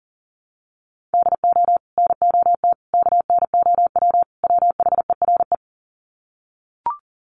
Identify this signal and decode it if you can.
Morse code, heard as a keyed tone, about 38 wpm, 700 Hz. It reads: DO NOT KNOW WHERE
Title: Sets of 4 Words 40wpm